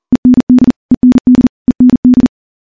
Morse code, heard as a keyed tone, about 40 wpm, 250 Hz. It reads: END END END
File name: morse-training-40WPM